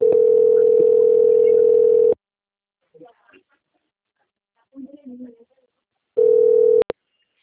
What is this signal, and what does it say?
Ringback tone (North America), 440 + 480 Hz, with speech.
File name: out- - - - - .85723